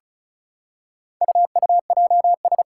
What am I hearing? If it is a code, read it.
Morse code, heard as a keyed tone, about 36 wpm, 700 Hz. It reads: UUJS
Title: Four Challenging Letter-Number Combinations 35wpm